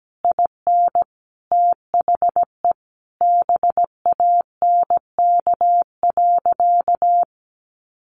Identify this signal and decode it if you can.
Morse code, heard as a keyed tone, about 17 wpm, 700 Hz. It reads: IN THE BANK.
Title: Sentences from Top 1200 Words 17wpm